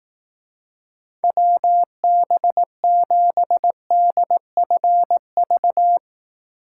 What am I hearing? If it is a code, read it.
Morse code, heard as a keyed tone, about 18 wpm, 700 Hz. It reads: WB7DFV
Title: Speed Racing - USA Call Signs 15wpm